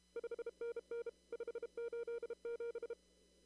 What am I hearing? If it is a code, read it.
Morse code, heard as a keyed tone, about 33 wpm, 470 Hz. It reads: 5NN 587
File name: CW / kt5j wpx cw-op-k5pi